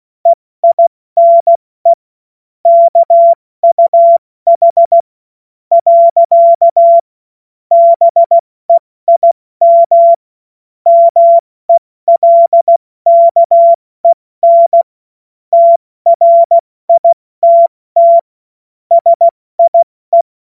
Morse code, heard as a keyed tone, about 16 wpm, spaced as for 14 wpm, 680 Hz. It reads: EINE KUH. BEIM MELKEN TRITT SIE
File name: Witz-22-Juli-2025-70-F80